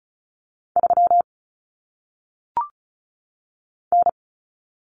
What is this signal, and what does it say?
Morse code, heard as a keyed tone, about 36 wpm, 700 Hz. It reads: 3 N
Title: Common Cut Numbers - 1-Digit 35wpm